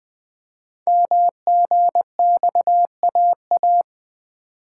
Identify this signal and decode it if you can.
Morse code, heard as a keyed tone, about 20 wpm, 700 Hz. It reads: MGXAA
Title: Code Groups - 5 Letters - 20wpm